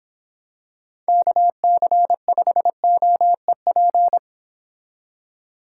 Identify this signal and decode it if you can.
Morse code, heard as a keyed tone, about 26 wpm, 700 Hz. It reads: KC5OEP
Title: Speed Racing - USA 2x3 Call Signs 22wpm